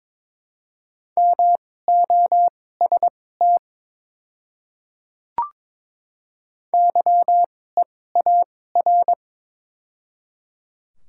Morse code, heard as a keyed tone, about 22 wpm, spaced as for 16 wpm, 700 Hz. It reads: MOST YEAR
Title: 2x Character Spacing - Top 200 Words - 22wpm